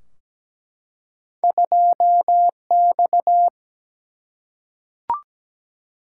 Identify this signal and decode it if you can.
Morse code, heard as a keyed tone, about 17 wpm, 700 Hz. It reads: 2X
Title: Single-Letter Single-Number 17wpm